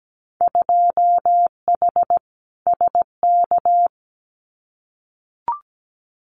Morse code, heard as a keyed tone, about 17 wpm, 700 Hz. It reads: 2H SK
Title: Winter Field Day Exchanges Encore 17wpm